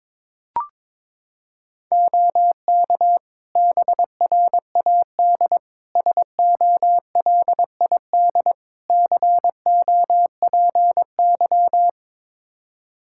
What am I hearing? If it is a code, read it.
Morse code, heard as a keyed tone, about 22 wpm, 700 Hz. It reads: OK BRAD SOLID COPY
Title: Practice QSO - Exchange 3 - More Info - 22wpm